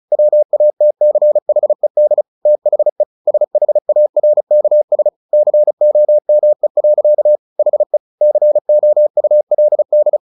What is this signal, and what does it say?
Morse code, heard as a keyed tone, about 35 wpm, 600 Hz. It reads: WATCHED THE SHARKS COME. HE COULD
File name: the old man and the sea 2 35wpm